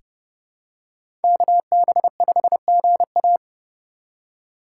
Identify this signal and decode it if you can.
Morse code, heard as a keyed tone, about 30 wpm, 700 Hz. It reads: KB5GA
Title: K1USN SST QSO Practice - 30wpm